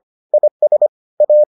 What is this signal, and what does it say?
Morse code, heard as a keyed tone, about 25 wpm, 600 Hz. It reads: IS A